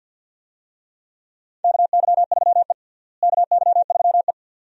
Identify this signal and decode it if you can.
Morse code, heard as a keyed tone, about 51 wpm, 700 Hz. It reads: KY3E KY3E
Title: USA Call Signs - 4 Characters - Repeated 2x - 50wpm